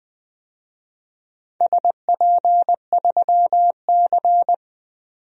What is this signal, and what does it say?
Morse code, heard as a keyed tone, about 20 wpm, 700 Hz. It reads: SP3C
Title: DXCC-100 Call Signs - 4 Characters - 20wpm